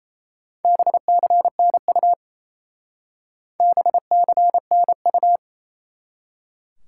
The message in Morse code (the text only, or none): BCNU BCNU